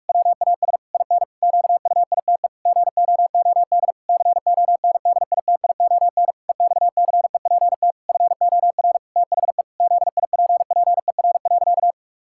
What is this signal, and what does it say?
Morse code, heard as a keyed tone, about 46 wpm, 700 Hz. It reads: WAS IN QUITE GOOD CONDITION EXCEPT FOR THE ZIPPER.
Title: Pangram Challenge v2 - 45wpm